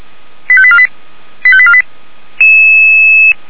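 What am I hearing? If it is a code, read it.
Five-tone selective call (ZVEI), heard as quick runs of five tones, tones of about 70 ms. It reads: 85848 85848
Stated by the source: ZVEI five-tone selective call received on an SDR.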